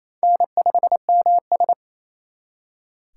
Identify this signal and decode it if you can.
Morse code, heard as a keyed tone, about 34 wpm, 700 Hz. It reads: N5MS
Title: Speed Racing - USA 1x2 Call Signs 28wpm